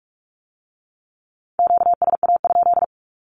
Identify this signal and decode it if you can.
Morse code, heard as a keyed tone, about 46 wpm, 700 Hz. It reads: QSA?
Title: Q-Codes with Spoken Meaning - 45wpm